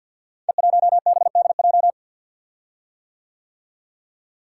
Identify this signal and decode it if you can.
Morse code, heard as a keyed tone, about 52 wpm, 690 Hz. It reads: E1BDJ
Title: Code Groups - 5 Letter:Numbers - 50wpm